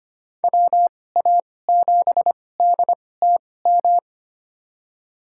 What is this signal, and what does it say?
Morse code, heard as a keyed tone, about 25 wpm, spaced as for 18 wpm, 700 Hz. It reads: WA7DTM
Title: USA Call Signs - 2x Character Spacing - 25wpm